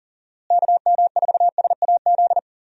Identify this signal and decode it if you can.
Morse code, heard as a keyed tone, about 39 wpm, 700 Hz. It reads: KM4SAZ
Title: USA Call Signs - 6 Characters - Repeated 2x - 40wpm